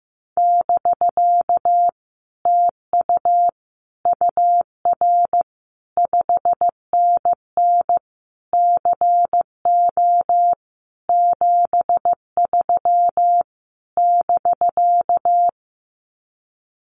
Morse code, heard as a keyed tone, about 15 wpm, 700 Hz. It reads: <BK> TU UR 5NN CO 73 <BK>
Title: Common QSO Phrases Encore 15wpm